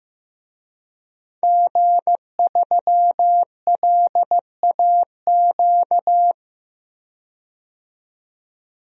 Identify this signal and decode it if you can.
Morse code, heard as a keyed tone, about 15 wpm, 700 Hz. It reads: G3LAQ
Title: DXCC-100 Call Signs - 5 Characters - 15wpm